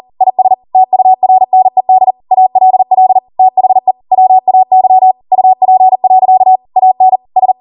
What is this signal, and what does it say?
Morse code, heard as a keyed tone, about 40 wpm, 750 Hz. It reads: IS TURNED ALL THE WAY UP. AN S
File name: Morse / 40WPM